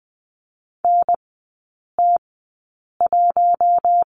Morse code, heard as a keyed tone, about 20 wpm, spaced as for 8 wpm, 700 Hz. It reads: NT1
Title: 100 Lesson 17e - Letter u - Three Characters 20wpm